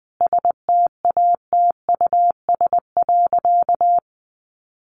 Morse code, heard as a keyed tone, about 20 wpm, 700 Hz. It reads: STATUS.
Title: Sentences from Top 1400 Words 20wpm